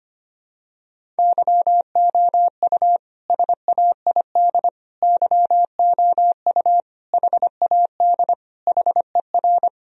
Morse code, heard as a keyed tone, about 25 wpm, 700 Hz. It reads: YOU SAID YOU HAD HER
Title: Sentences from Top 1000 Words - Review Part 1 - 25wpm